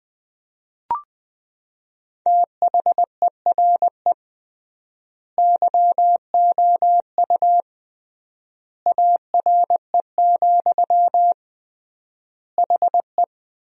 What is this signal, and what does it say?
Morse code, heard as a keyed tone, about 20 wpm, 700 Hz. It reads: THERE YOU ARE, HE
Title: 3x Word-Spacing - Sentences from Top 100 Words - 20wpm